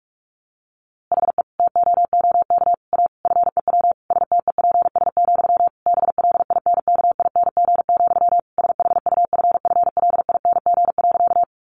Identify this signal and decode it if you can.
Morse code, heard as a keyed tone, about 44 wpm, 700 Hz. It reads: HE TOOK A FEW STEPS, BLINKING, SHUFFLING.
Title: A Sound Of Thunder 45wpm